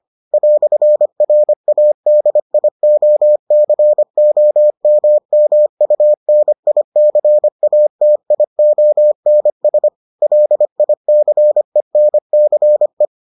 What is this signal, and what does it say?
Morse code, heard as a keyed tone, about 25 wpm, 600 Hz. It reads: "RADIOCOMMUNICATIONS LICENCE